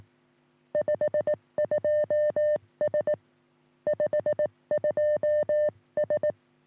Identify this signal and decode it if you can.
Morse code, heard as a keyed tone, about 18 wpm, 590 Hz. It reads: 52S 52S
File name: sm7lcb / 1941 sm0err 10g rs